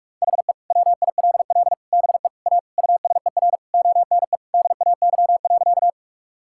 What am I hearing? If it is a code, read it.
Morse code, heard as a keyed tone, about 41 wpm, 700 Hz. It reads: HE WILL BE A USER ONE DAY.
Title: Sentences from Top 1000 Words - Review Part 5 - 45wpm